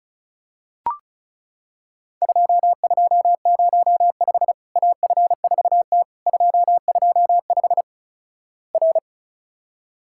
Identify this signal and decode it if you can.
Morse code, heard as a keyed tone, about 36 wpm, 700 Hz. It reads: 2205 AF4T 225 R
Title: Worked All Europe DX Contest - QTC Exchange - 35wpm